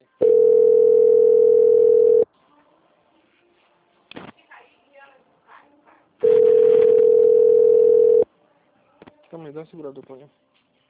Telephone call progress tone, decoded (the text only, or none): ringback tone